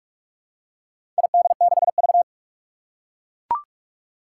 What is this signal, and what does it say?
Morse code, heard as a keyed tone, about 41 wpm, 700 Hz. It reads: ID6V
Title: Four Challenging Letter-Number Combinations 45wpm